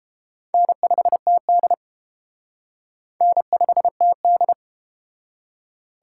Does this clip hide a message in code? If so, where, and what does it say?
Yes: Morse code from 0.5 to 4.5 s, heard as a keyed tone, about 31 wpm, 700 Hz. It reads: N5TD N5TD